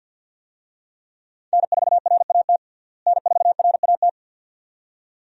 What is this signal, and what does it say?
Morse code, heard as a keyed tone, about 51 wpm, 700 Hz. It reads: N4RAT N4RAT
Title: USA Call Signs - 5 Characters - Repeated 2x - 50wpm